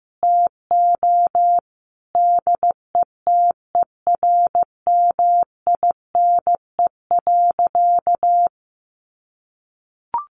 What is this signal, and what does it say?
Morse code, heard as a keyed tone, about 14 wpm, 700 Hz. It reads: TO DETERMINE.
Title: Sentences from Top 1500 Words 15wpm